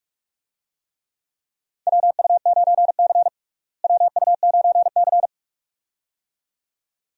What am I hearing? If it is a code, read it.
Morse code, heard as a keyed tone, about 46 wpm, 700 Hz. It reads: WU9C WU9C
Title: USA Call Signs - 4 Characters - Repeated 2x - 45wpm